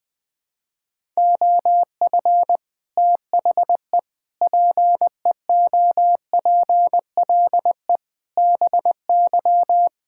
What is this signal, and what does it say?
Morse code, heard as a keyed tone, about 20 wpm, 700 Hz. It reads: OF THE PEOPLE BY
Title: Sentences with 8-Word Phrases - 20wpm